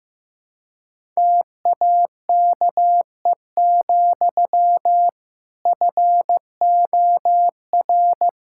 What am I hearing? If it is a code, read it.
Morse code, heard as a keyed tone, about 15 wpm, 700 Hz. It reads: TAKE, FOR
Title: Sentences from Top 1000 Words - Review Part 5 - 15wpm